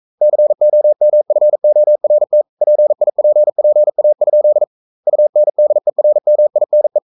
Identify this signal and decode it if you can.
Morse code, heard as a keyed tone, about 42 wpm, 600 Hz. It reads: COMFORT PIPPA? UNDERMINE
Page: Romanen Never af Ken Follet på engelsk.